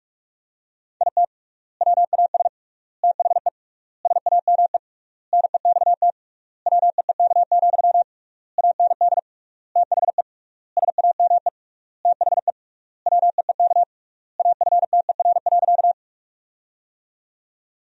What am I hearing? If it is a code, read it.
Morse code, heard as a keyed tone, about 47 wpm, 700 Hz. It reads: IT WAS THE SAME NEXT WEEK, AND THE SAME THE WEEK AFTER.
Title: 3x Word-Spacing - Sentences from Top 500 Words - 45wpm